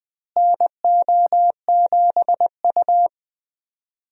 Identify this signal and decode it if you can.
Morse code, heard as a keyed tone, about 20 wpm, 700 Hz. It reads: NO7U
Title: USA Call Signs - 4 Characters - Repeated 2x - 20wpm